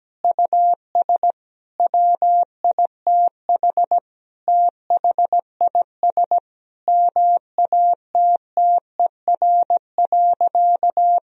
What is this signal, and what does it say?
Morse code, heard as a keyed tone, about 17 wpm, 700 Hz. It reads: US WITH THIS MATTER.
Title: Sentences from Top 1000 Words - Review Part 2 - 17wpm